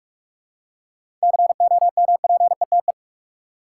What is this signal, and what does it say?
Morse code, heard as a keyed tone, about 43 wpm, 700 Hz. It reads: COMPETE
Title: Seven Letter Words - 45wpm